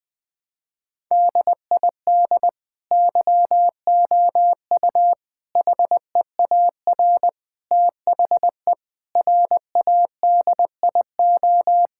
Morse code, heard as a keyed tone, about 20 wpm, 700 Hz. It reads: DID YOU HEAR THE RADIO
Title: Sentences from Top 1000 Words - Review Part 5 - 20wpm